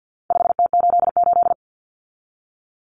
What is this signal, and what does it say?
Morse code, heard as a keyed tone, about 50 wpm, 700 Hz. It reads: 5T88
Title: Common Cut Numbers - 4-Digit 50wpm